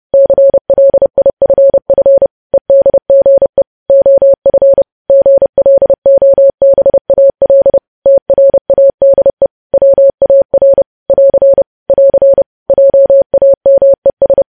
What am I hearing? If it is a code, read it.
Morse code, heard as a keyed tone, about 30 wpm, 550 Hz. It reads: CLIFF EDGE OF GLOBAL TRADE WAR <AR> <AR> JAMES
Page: news headlines in morse code at 30 wpm